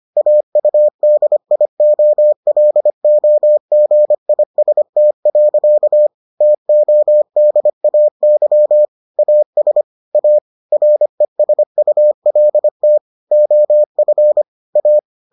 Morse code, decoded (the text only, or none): AUDIOLOGIST. TODAY AS A RESULT OF A